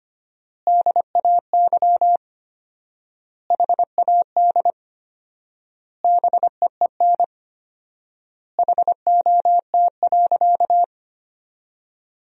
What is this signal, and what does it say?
Morse code, heard as a keyed tone, about 25 wpm, 700 Hz. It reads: DAY HAD BEEN HOT.
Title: 4x Word-Spacing - Sentences from Top 500 Words - 25wpm